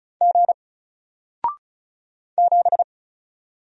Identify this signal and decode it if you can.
Morse code, heard as a keyed tone, about 35 wpm, 700 Hz. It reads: G 7
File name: Single-Letter-Number-35wpm